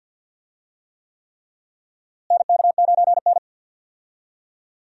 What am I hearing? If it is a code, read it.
Morse code, heard as a keyed tone, about 46 wpm, 700 Hz. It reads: NK9N
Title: USA Call Signs - 4 Characters - Repeated 3x - 50wpm